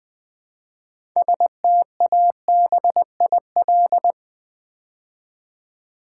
Morse code, heard as a keyed tone, about 20 wpm, 700 Hz. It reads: STABIL
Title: Build A Word - 3 Letters At A Time - 9-Letter Words - 20wpm